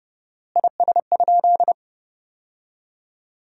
Morse code, heard as a keyed tone, about 32 wpm, 700 Hz. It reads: IS?
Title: 4x Word-Spacing - Sentences from Top 300 Words - 30wpm